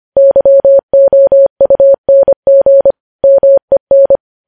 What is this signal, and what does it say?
Morse code, heard as a keyed tone, about 25 wpm, 550 Hz. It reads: YOUNG MEN
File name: podcast / News headlines-1x-25x25